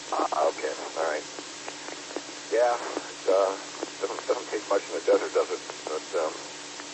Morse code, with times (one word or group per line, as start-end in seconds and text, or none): none